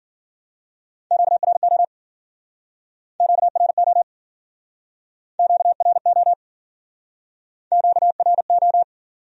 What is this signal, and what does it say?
Morse code, heard as a keyed tone, about 52 wpm, 700 Hz. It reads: QRO QRO QRO QRO